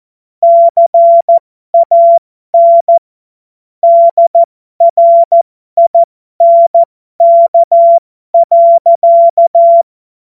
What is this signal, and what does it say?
Morse code, heard as a keyed tone, about 14 wpm, spaced as for 12 wpm, 680 Hz. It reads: CAN DRINK.
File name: Morsekurs / Witze